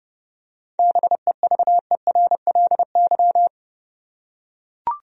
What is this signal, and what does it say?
Morse code, heard as a keyed tone, about 32 wpm, 700 Hz. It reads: BEVERLY
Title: USA Common Names - 30wpm